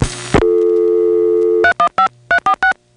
DTMF, heard as telephone touch tones, tones of about 80 ms. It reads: 315346